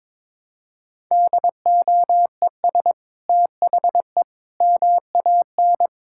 Morse code, heard as a keyed tone, about 22 wpm, 700 Hz. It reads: DOES THE MAN
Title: Sentences from Top 1000 Words - Review Part 5 - 22wpm